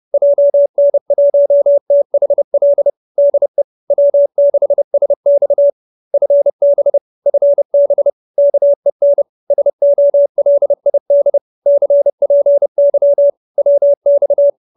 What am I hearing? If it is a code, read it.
Morse code, heard as a keyed tone, about 30 wpm, 570 Hz. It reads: JN1THL DE W6SX FB FB KEN SOLID CPY WX